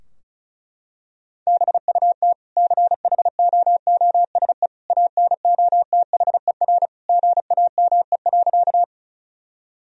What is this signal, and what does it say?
Morse code, heard as a keyed tone, about 34 wpm, 700 Hz. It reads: BUT CHOOSE ANOTHER GAME.
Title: Sentences from Top 1000 Words - Review Part 4 - 35wpm